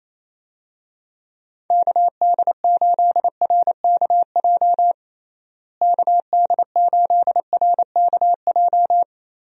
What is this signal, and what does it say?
Morse code, heard as a keyed tone, about 27 wpm, 700 Hz. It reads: KD8RKJ KD8RKJ
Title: USA Call Signs - 6 Characters - Repeated 3x - 28wpm